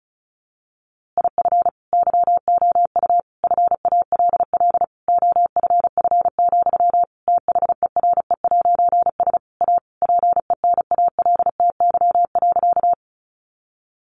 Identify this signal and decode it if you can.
Morse code, heard as a keyed tone, about 34 wpm, 700 Hz. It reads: IF YOU FALL OFF, THERE'S A PENALTY.
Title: A Sound Of Thunder 35wpm